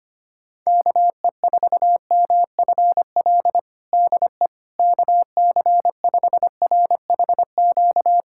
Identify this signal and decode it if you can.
Morse code, heard as a keyed tone, about 25 wpm, 700 Hz. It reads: KE4MFL DE KC5RHQ
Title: USA Callsign DE Callsign 25wpm